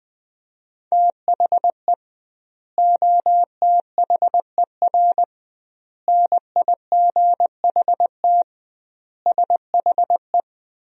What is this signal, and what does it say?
Morse code, heard as a keyed tone, about 20 wpm, 700 Hz. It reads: THE OTHER NIGHT SHE
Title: 2x Word-Spacing - Sentences from Top 300 Words - 20wpm